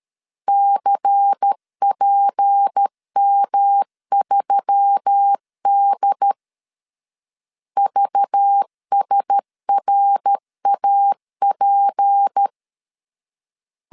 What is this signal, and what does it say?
Morse code, heard as a keyed tone, about 13 wpm, 800 Hz. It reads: CPM3D VSRAP
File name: CW / DET / 1 9 1500C S C 12wpm 3811